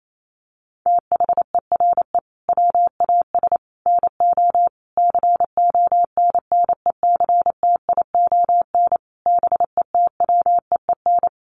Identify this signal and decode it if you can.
Morse code, heard as a keyed tone, about 28 wpm, 700 Hz. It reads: THERE WAS NO CONNECTION BETWEEN
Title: Sentences from Top 1300 Words 28wpm